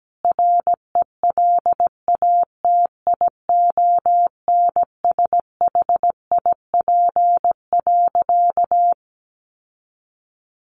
Morse code, heard as a keyed tone, about 17 wpm, 700 Hz. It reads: RELATIONSHIP.
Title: Sentences from Top 1100 Words 17wpm